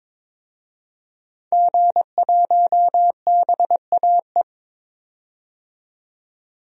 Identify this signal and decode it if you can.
Morse code, heard as a keyed tone, about 22 wpm, 700 Hz. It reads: G1BAE
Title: DXCC-100 Call Signs - 5 Characters - 22wpm